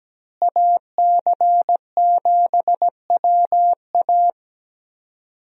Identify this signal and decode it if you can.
Morse code, heard as a keyed tone, about 17 wpm, 700 Hz. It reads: AC7WA
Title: USA Call Signs - 5 Characters - Repeated 2x - 17wpm